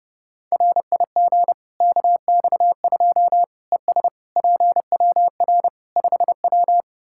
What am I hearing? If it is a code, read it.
Morse code, heard as a keyed tone, about 32 wpm, 700 Hz. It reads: RIG KX2 ES PWR 5W
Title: K1USN SST QSO Practice 30wpm